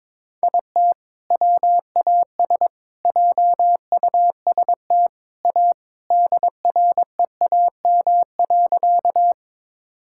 Morse code, heard as a keyed tone, about 22 wpm, 700 Hz. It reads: IT WAS JUST A DREAM.
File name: practice / 22wpm / Sentences with 3-Word Phrases - 22wpm